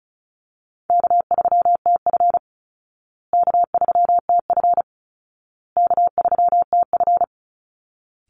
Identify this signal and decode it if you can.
Morse code, heard as a keyed tone, about 34 wpm, 700 Hz. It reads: K3TF K3TF K3TF
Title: Easy Call Signs Repeated 3x - 35wpm